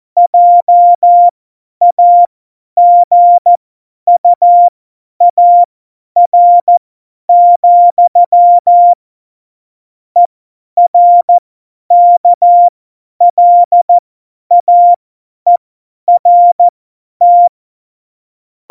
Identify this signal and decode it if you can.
Morse code, heard as a keyed tone, about 14 wpm, spaced as for 10 wpm, 700 Hz. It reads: JAGUAR, ERKLAERT
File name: Morsekurs / Witze / Witz 19. September 17, Tempo 50